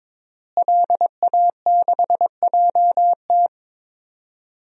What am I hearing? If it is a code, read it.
Morse code, heard as a keyed tone, about 22 wpm, 700 Hz. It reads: LA6JT
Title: DXCC-100 Call Signs - 5 Characters - Repeated 3x - 22wpm